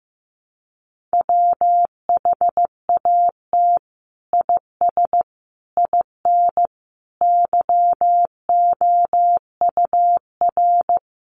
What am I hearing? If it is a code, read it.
Morse code, heard as a keyed tone, about 15 wpm, 700 Hz. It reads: WHAT IS IN YOUR
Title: Sentences from Top 200 Words 15wpm